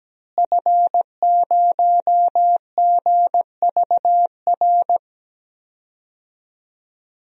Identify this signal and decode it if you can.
Morse code, heard as a keyed tone, about 17 wpm, 700 Hz. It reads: F0GVR
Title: DXCC-100 Call Signs - 5 Characters - Repeated 3x - 17wpm